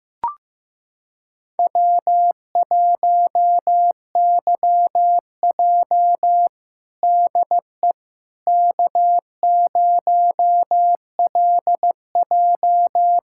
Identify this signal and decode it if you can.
Morse code, heard as a keyed tone, about 15 wpm, 700 Hz. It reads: W1YJ DE K0LJ
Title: USA Callsign DE Callsign - 4 Characters - 15wpm